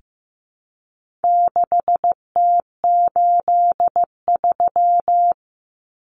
Morse code, heard as a keyed tone, about 15 wpm, 700 Hz. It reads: 6T83
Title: Common Cut Numbers - 4-Digit 15wpm